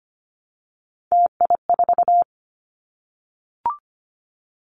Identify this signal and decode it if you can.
Morse code, heard as a keyed tone, about 25 wpm, 700 Hz. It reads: TI4
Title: Callsign Prefixes for Top 100 DXCC - 25wpm